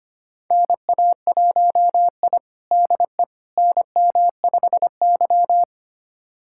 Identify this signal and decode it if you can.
Morse code, heard as a keyed tone, about 25 wpm, 700 Hz. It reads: NA1I DE NM5Y
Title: USA Callsign DE Callsign - 4 Characters - 25wpm